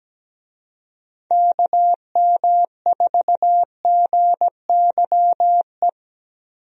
Morse code, heard as a keyed tone, about 17 wpm, 700 Hz. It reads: KM4GYE
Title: USA Call Signs - 6 Characters - Repeated 3x - 17wpm